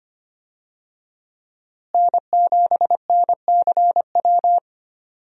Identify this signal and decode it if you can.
Morse code, heard as a keyed tone, about 25 wpm, 700 Hz. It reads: N7NCW
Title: USA Call Signs - 5 Characters - Repeated 2x - 25wpm